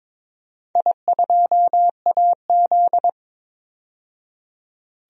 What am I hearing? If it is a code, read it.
Morse code, heard as a keyed tone, about 22 wpm, 700 Hz. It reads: I2AZ
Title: DXCC-100 Call Signs - 4 Characters - 22wpm